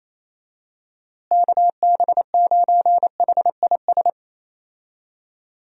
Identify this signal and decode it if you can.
Morse code, heard as a keyed tone, about 27 wpm, 700 Hz. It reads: KB9HIS
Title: USA Call Signs - 6 Characters - Repeated 2x - 28wpm